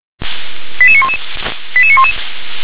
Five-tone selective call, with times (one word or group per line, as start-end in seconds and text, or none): 0.8-1.2 88911
1.8-2.1 88911